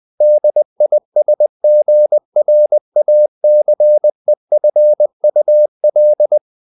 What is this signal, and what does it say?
Morse code, heard as a keyed tone, about 20 wpm, 600 Hz. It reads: DISGRACEFUL
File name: the old man and the sea / the old man and the sea part 1 20wpm